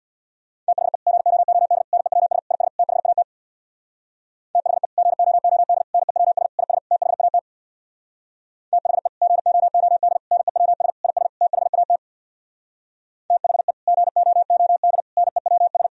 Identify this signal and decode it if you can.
Morse code, heard as a keyed tone, about 63 wpm, 700 Hz. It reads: THE GOOD NEWS IS THAT THE GOOD NEWS IS THAT THE GOOD NEWS IS THAT THE GOOD NEWS